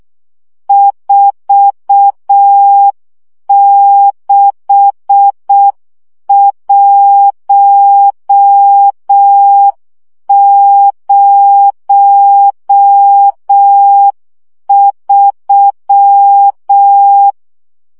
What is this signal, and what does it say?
Morse code, heard as a keyed tone, about 6 wpm, 800 Hz. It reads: 46103
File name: morze vizsga 2011-1 5